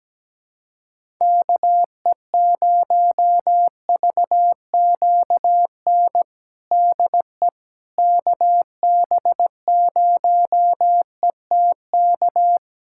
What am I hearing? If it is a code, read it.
Morse code, heard as a keyed tone, about 17 wpm, 700 Hz. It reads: KE0VQN DE KB0ETK
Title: USA Callsign DE Callsign 17wpm